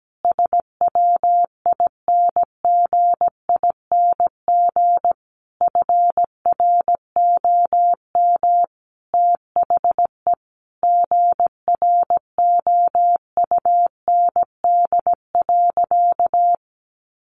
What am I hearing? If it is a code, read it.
Morse code, heard as a keyed tone, about 17 wpm, 700 Hz. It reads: SWINGING FROM THE GROUND.
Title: Easy Phrases 17wpm